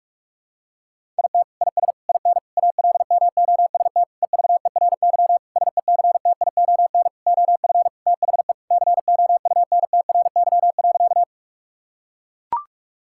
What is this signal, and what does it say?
Morse code, heard as a keyed tone, about 41 wpm, 700 Hz. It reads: IT IS IN ALMOST EVERY SECTION OF THE COUNTRY.
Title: Sentences from Top 1000 Words - Review Part 5 - 45wpm